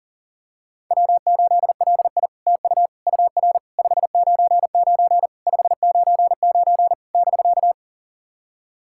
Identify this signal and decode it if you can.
Morse code, heard as a keyed tone, about 40 wpm, 690 Hz. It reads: W8LI TU UR 599 599 <BK>
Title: SOTA QSO Practice 40wpm